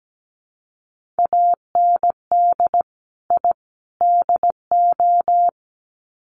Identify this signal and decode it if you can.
Morse code, heard as a keyed tone, about 17 wpm, 700 Hz. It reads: AND I DO
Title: Sets of 3 Words 17wpm